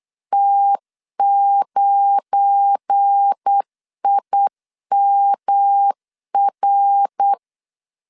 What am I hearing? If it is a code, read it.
Morse code, heard as a keyed tone, about 8 wpm, 800 Hz. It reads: T9IMR
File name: LEZIONI / 10 SIGNS / 10C DOT 8wpm 3649